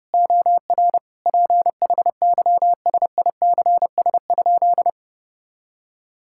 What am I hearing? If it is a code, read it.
Morse code, heard as a keyed tone, about 30 wpm, 700 Hz. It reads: OR PHYSICS?
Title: Sentences from Top 1000 Words - Review Part 6 - 30wpm